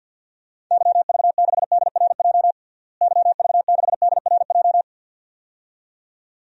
Morse code, heard as a keyed tone, about 52 wpm, 700 Hz. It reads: YV6DRJ YV6DRJ
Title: DXCC-100 Call Signs - 6 Characters - Repeated 2x - 50wpm